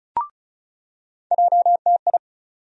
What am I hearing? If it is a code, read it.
Morse code, heard as a keyed tone, about 36 wpm, 700 Hz. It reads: JTI
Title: USA Call Sign Postfixes - 3 Letters 35wpm